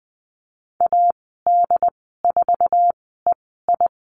Morse code, heard as a keyed tone, about 20 wpm, spaced as for 15 wpm, 700 Hz. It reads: AD4EI
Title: 069 Lesson 12g - Letter d - Call Signs 20wpm